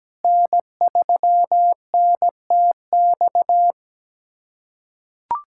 Morse code, heard as a keyed tone, about 17 wpm, 700 Hz. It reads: N3NTX
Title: USA Callsign DE Callsign - 5 Characters - 17wpm